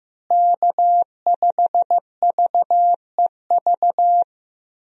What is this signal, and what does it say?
Morse code, heard as a keyed tone, about 15 wpm, 700 Hz. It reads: K5VEV
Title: USA Call Signs - 5 Characters - Repeated 3x - 15wpm